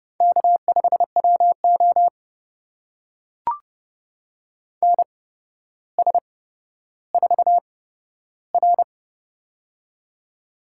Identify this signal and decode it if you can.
Morse code, heard as a keyed tone, about 30 wpm, spaced as for 13 wpm, 700 Hz. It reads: K5WO NS4R
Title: USA Call Signs - 4 Characters - 8x Character Spacing - 30wpm